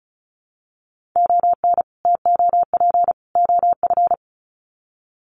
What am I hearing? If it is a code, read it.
Morse code, heard as a keyed tone, about 34 wpm, 700 Hz. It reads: ON TOP OF
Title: Sets of 3 Words 35wpm